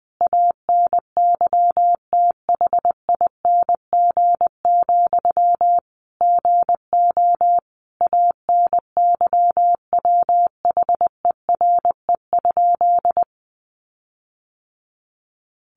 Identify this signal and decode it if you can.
Morse code, heard as a keyed tone, about 20 wpm, 700 Hz. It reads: ANYTHING, GO ANYWHERE?
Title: Sentences from Top 1400 Words 20wpm